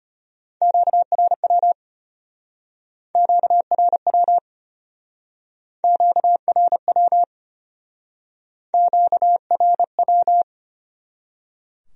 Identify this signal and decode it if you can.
Morse code, heard as a keyed tone, about 33 wpm, 700 Hz. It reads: QRW QRW QRW QRW